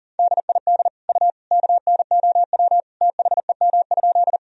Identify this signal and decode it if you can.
Morse code, heard as a keyed tone, about 40 wpm, 700 Hz. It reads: DID U KNOW THEM?
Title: Sentences from Top 100 Words and Abbreviations 40wpm